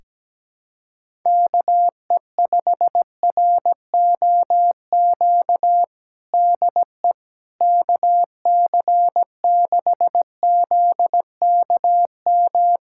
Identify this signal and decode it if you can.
Morse code, heard as a keyed tone, about 17 wpm, 700 Hz. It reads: KE5ROQ DE KC6ZKM
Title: USA Callsign DE Callsign - 6 Characters - 17wpm